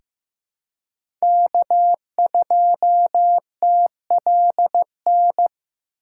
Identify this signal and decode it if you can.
Morse code, heard as a keyed tone, about 15 wpm, 700 Hz. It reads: K2TLN
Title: USA Call Signs - 5 Characters - 3x Character Spacing - 15wpm